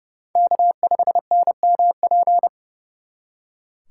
Morse code, heard as a keyed tone, about 36 wpm, 700 Hz. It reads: K5NMP
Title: Speed Racing - USA 1x3 Call Signs 30wpm